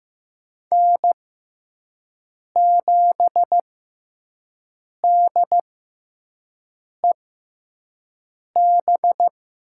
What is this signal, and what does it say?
Morse code, heard as a keyed tone, about 15 wpm, spaced as for 5 wpm, 700 Hz. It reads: N7DEB
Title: USA Call Signs - 5 Characters - 6x Character Spacing - 15wpm